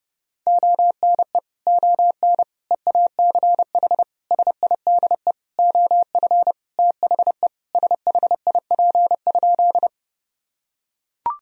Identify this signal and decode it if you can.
Morse code, heard as a keyed tone, about 30 wpm, 700 Hz. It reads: ONE ON EACH SIDE OF THE SHIP?
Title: Sentences from Top 1000 Words - Review Part 2 - 30wpm